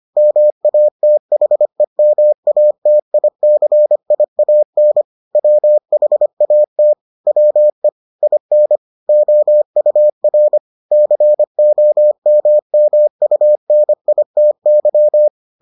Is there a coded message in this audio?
Yes: Morse code from 0.2 to 15.3 s, heard as a keyed tone, about 25 wpm, 600 Hz. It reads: MATHEMATICIAN WHAT WE IN OUR COMMUNITY